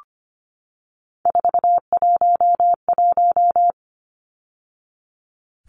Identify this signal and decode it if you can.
Morse code, heard as a keyed tone, about 25 wpm, 690 Hz. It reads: 411
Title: Numbers - 3-Digit 25wpm